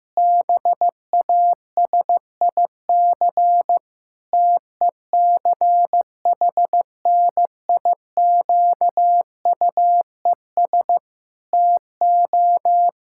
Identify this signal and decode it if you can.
Morse code, heard as a keyed tone, about 16 wpm, 700 Hz. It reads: BASIC TECHNIQUES TO
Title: Pangram Challenge v2 - 15wpm